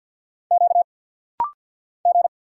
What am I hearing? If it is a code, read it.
Morse code, heard as a keyed tone, about 50 wpm, 690 Hz. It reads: Q G
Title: Single Letters - Rapid-Fire 50wpm